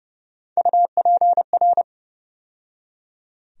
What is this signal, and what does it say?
Morse code, heard as a keyed tone, about 32 wpm, 700 Hz. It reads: UPR
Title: USA Call Sign Postfixes - 3 Letters 30wpm